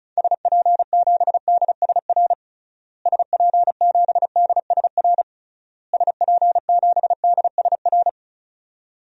Morse code, heard as a keyed tone, about 36 wpm, 700 Hz. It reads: SP7DSR SP7DSR SP7DSR
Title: DXCC-100 Call Signs - 6 Characters - Repeated 3x - 35wpm